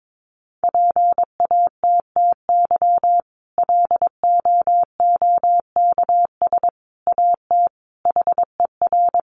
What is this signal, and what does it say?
Morse code, heard as a keyed tone, about 22 wpm, 700 Hz. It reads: PATTY LOOKS AT HER
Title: Additional Easy Phrases 22wpm